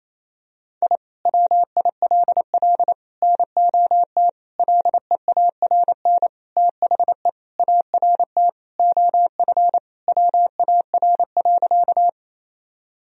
Morse code, heard as a keyed tone, about 28 wpm, 700 Hz. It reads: I WILL NOT LEARN THE ART OF WAR.
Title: Sentences from Top 1000 Words - Review Part 3 - 28wpm